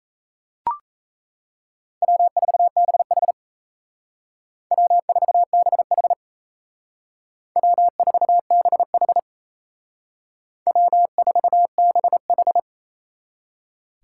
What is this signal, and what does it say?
Morse code, heard as a keyed tone, about 36 wpm, 700 Hz. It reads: W4BH W4BH W4BH W4BH